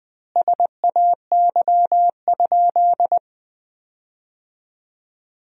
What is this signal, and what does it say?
Morse code, heard as a keyed tone, about 21 wpm, 700 Hz. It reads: SAY?
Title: 3x Word-Spacing - Sentences from Top 300 Words - 20wpm